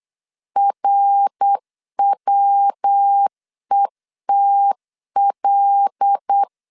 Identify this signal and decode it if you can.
Morse code, heard as a keyed tone, about 8 wpm, 800 Hz. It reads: RWETL
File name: LEZIONI / 8 YJ45 / 8 Y 8wpm 3245